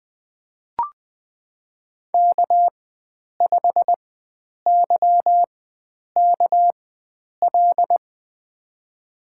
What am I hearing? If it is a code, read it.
Morse code, heard as a keyed tone, about 21 wpm, spaced as for 9 wpm, 700 Hz. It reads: K5YKL
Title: USA Call Signs - 5 Characters - 4x Character Spacing - 20wpm